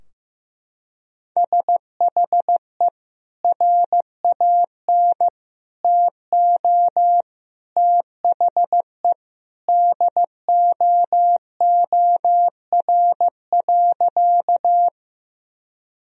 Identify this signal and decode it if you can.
Morse code, heard as a keyed tone, about 16 wpm, 700 Hz. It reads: SHE RAN TO THE DOOR.
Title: Sentences from Top 1000 Words - Review Part 3 - 15wpm